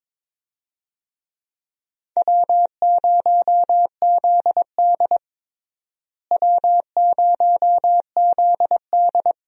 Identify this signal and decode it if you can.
Morse code, heard as a keyed tone, about 22 wpm, 700 Hz. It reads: W0ZD W0ZD
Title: USA Call Signs - 4 Characters - Repeated 2x - 22wpm